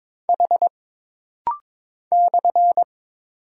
Morse code, heard as a keyed tone, about 22 wpm, 700 Hz. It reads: H /
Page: This rapid-fire timing is geared to push your ICR (Instant Character Recognition) skills.